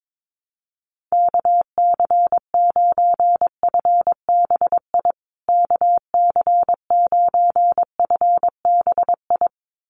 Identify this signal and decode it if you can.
Morse code, heard as a keyed tone, about 22 wpm, 700 Hz. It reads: KC9FBI KC9FBI
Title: USA Call Signs Repeated 2x 22wpm